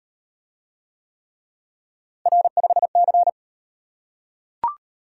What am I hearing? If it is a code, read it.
Morse code, heard as a keyed tone, about 38 wpm, 700 Hz. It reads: R5C